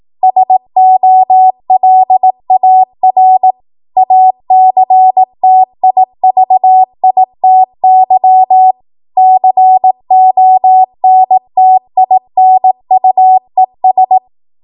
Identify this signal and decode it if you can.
Morse code, heard as a keyed tone, about 18 wpm, 750 Hz. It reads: SOLAR ACTIVITY CONTINUES
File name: Archive / 18 WPM / 18WPM